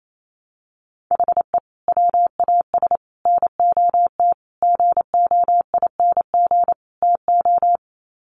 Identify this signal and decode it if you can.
Morse code, heard as a keyed tone, about 28 wpm, 700 Hz. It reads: HE WAS NOT GOING TO
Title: Sets of 5 Words 28wpm